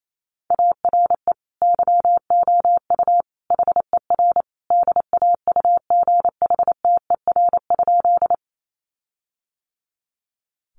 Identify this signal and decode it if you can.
Morse code, heard as a keyed tone, about 28 wpm, 700 Hz. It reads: ARE YOU HER DAUGHTER?
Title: Sentences from Top 1500 Words 28wpm